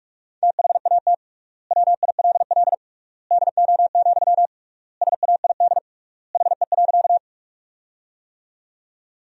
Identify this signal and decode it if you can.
Morse code, heard as a keyed tone, about 46 wpm, 700 Hz. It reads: THAT WILL DO, SAID HE.
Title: 3x Word-Spacing - Sentences from Top 100 Words - 45wpm